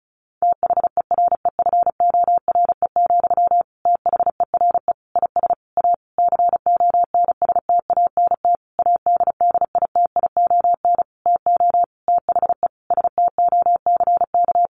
Morse code, heard as a keyed tone, about 34 wpm, 700 Hz. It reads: THEREFORE, THERE IS A CONSTANT ADDITION TO THE STOCK
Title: Sentences from Top 1500 Words 35wpm